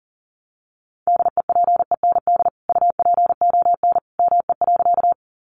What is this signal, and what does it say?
Morse code, heard as a keyed tone, about 40 wpm, 700 Hz. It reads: DEPEND UPON ME.
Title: Sentences from Top 1300 Words 40wpm